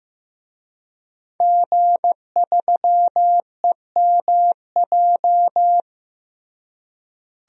Morse code, heard as a keyed tone, about 15 wpm, 700 Hz. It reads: G3EMJ
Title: DXCC-100 Call Signs - 5 Characters - Repeated 3x - 15wpm